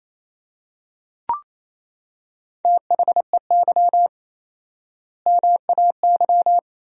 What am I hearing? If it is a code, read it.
Morse code, heard as a keyed tone, about 28 wpm, 700 Hz. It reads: THEY MAY
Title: 4x Word-Spacing - Sentences from Top 300 Words - 28wpm